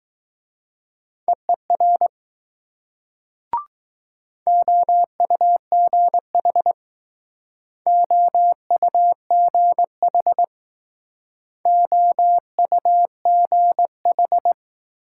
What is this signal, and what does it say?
Morse code, heard as a keyed tone, about 19 wpm, 700 Hz. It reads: EER OUGH OUGH OUGH